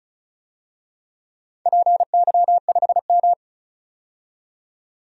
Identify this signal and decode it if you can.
Morse code, heard as a keyed tone, about 34 wpm, 700 Hz. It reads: PY5M
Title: DXCC-100 Call Signs - 4 Characters - 35wpm